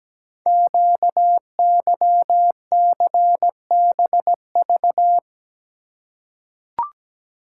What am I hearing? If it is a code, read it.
Morse code, heard as a keyed tone, about 17 wpm, 700 Hz. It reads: QYCBV
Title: Code Groups - 5 Letters - 17wpm